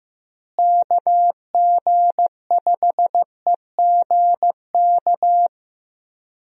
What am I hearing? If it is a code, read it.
Morse code, heard as a keyed tone, about 16 wpm, 700 Hz. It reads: KG5EGK
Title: USA Call Signs - 6 Characters - Repeated 2x - 15wpm